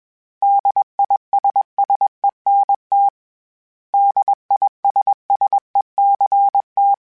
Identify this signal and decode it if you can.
Morse code, heard as a keyed tone, about 21 wpm, 800 Hz. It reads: DISSENT DISSECT
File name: General / General-mp3s / Suffixes @20